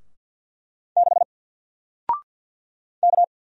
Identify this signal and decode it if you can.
Morse code, heard as a keyed tone, about 54 wpm, 700 Hz. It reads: 6 K
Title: Single Letter-Number Alt 50wpm